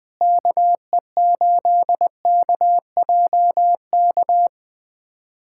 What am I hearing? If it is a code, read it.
Morse code, heard as a keyed tone, about 20 wpm, 700 Hz. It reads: KE8KJK
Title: USA Callsign DE Callsign 20wpm